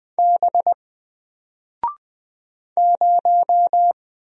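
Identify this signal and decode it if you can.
Morse code, heard as a keyed tone, about 20 wpm, 700 Hz. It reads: B 0
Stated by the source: Single letter and number Morse drill at 20 wpm, from N4LCD.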